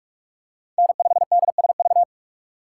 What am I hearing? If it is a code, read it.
Morse code, heard as a keyed tone, about 41 wpm, 700 Hz. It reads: N5DSV
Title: USA Call Signs - 5 Characters - 3x Character Spacing - 45wpm